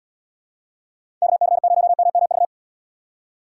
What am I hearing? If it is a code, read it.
Morse code, heard as a keyed tone, about 72 wpm, 700 Hz. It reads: KC0MMV